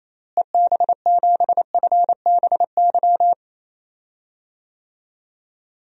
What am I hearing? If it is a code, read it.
Morse code, heard as a keyed tone, about 28 wpm, 700 Hz. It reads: EB7FBY
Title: DXCC-100 Call Signs - 6 Characters - Repeated 3x - 28wpm